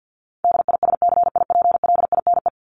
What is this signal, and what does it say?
Morse code, heard as a keyed tone, about 50 wpm, 700 Hz. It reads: DISCIPLINE
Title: Top 751-1000 Words 50wpm